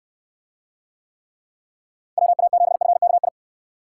Morse code, heard as a keyed tone, about 69 wpm, 700 Hz. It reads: WA7FKI